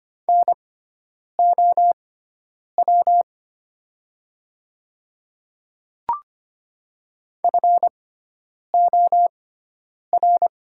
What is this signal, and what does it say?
Morse code, heard as a keyed tone, about 25 wpm, spaced as for 9 wpm, 700 Hz. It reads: NOW FOR
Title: 6x Character Spacing - Top 100 Words - 25wpm